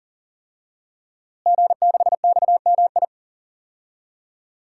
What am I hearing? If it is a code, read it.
Morse code, heard as a keyed tone, about 40 wpm, 700 Hz. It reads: G6XMI
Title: DXCC-100 Call Signs - 5 Characters - Repeated 2x - 40wpm